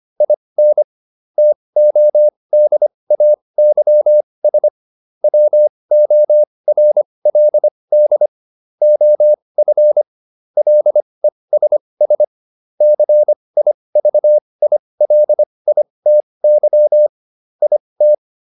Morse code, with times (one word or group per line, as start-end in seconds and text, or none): 0.2-0.8 IN
1.4-4.7 TODAYS
5.2-8.3 WORLD
8.8-10.0 OF
10.6-12.2 LESS
12.8-17.1 CIVILITY
17.6-18.1 IT